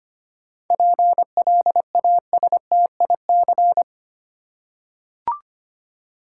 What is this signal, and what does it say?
Morse code, heard as a keyed tone, about 25 wpm, 700 Hz. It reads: PLASTIC
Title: Seven Letter Words - 25wpm